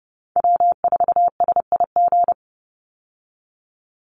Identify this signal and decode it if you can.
Morse code, heard as a keyed tone, about 30 wpm, 700 Hz. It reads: W4SIG
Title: Call Signs - 30wpm